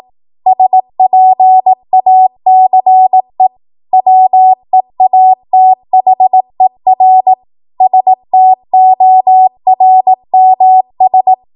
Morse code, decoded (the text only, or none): SPACE WEATHER STORMS